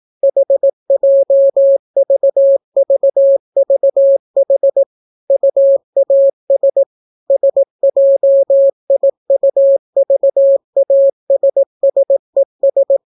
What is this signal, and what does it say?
Morse code, heard as a keyed tone, about 18 wpm, 550 Hz. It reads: HJVVVH UAS SJIUVASSES